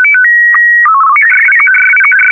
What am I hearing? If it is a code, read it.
SSTV (slow-scan television), Martin 1.